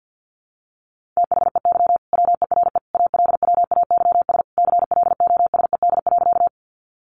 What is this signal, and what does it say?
Morse code, heard as a keyed tone, about 50 wpm, 700 Hz. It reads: THEY WERE ALWAYS CLOSED.
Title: Sentences from Top 1400 Words 50wpm